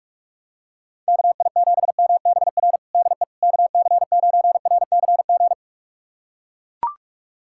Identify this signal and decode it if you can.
Morse code, heard as a keyed tone, about 41 wpm, 700 Hz. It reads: KI7MBR DE KC9RCG
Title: USA Callsign DE Callsign - 6 Characters - 45wpm